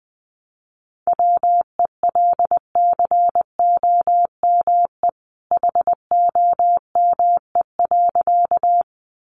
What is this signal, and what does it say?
Morse code, heard as a keyed tone, about 20 wpm, 700 Hz. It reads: WELCOME HOME.
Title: A Sound Of Thunder 20wpm